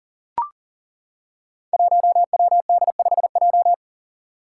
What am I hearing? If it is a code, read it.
Morse code, heard as a keyed tone, about 40 wpm, 700 Hz. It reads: 1WD5J
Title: Five Challenging Letter-Number Combinations 40wpm